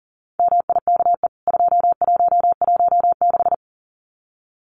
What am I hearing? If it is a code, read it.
Morse code, heard as a keyed tone, about 40 wpm, 700 Hz. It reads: MIKE 2116
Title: CWT Exchange 40wpm